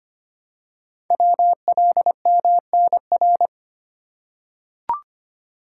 Morse code, heard as a keyed tone, about 25 wpm, 700 Hz. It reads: WLMNR
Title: Code Groups - 5 Letters - 25wpm